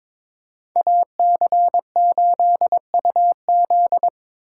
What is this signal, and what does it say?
Morse code, heard as a keyed tone, about 29 wpm, 700 Hz. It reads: AC8UZ